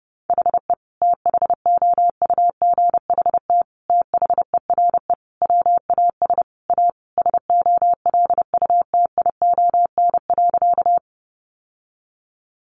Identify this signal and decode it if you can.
Morse code, heard as a keyed tone, about 30 wpm, 700 Hz. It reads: HE THOUGHT THERE WAS A SOLUTION.
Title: Sentences from Top 1300 Words 30wpm